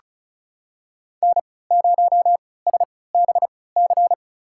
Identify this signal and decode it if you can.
Morse code, heard as a keyed tone, about 35 wpm, spaced as for 20 wpm, 700 Hz. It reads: N0SBC
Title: USA Call Signs - 5 Characters - 3x Character Spacing - 35wpm